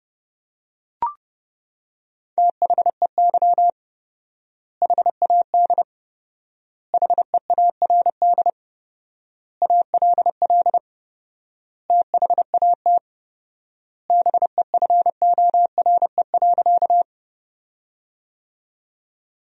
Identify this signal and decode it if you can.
Morse code, heard as a keyed tone, about 30 wpm, 700 Hz. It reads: THEY HAD HEARD ALL THAT BEFORE.
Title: 4x Word-Spacing - Sentences from Top 400 Words - 30wpm